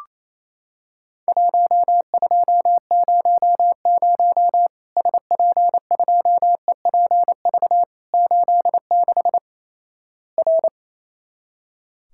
Morse code, heard as a keyed tone, about 28 wpm, 700 Hz. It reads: 1200 SP2EPV 86 R
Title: Worked All Europe DX Contest - QTC Exchange - 28wpm